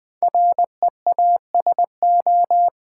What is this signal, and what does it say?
Morse code, heard as a keyed tone, about 20 wpm, 700 Hz. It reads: REASO
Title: Build A Word - 5 Letters At A Time - 10-Letter Words - 20wpm